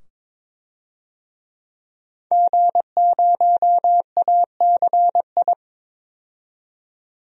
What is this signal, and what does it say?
Morse code, heard as a keyed tone, about 21 wpm, 700 Hz. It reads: G0ACI
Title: DXCC-100 Call Signs - 5 Characters - 22wpm